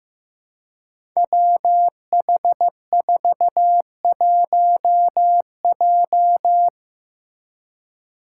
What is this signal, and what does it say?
Morse code, heard as a keyed tone, about 15 wpm, 700 Hz. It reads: WH41J
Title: Five Challenging Letter-Number Combinations 15wpm